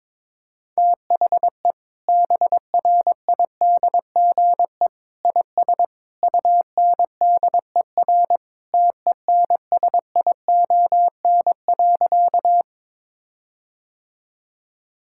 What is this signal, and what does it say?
Morse code, heard as a keyed tone, about 22 wpm, 700 Hz. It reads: THE BRIDGE IS UNDER TENSION.
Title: Sentences from Top 1600 Words 22wpm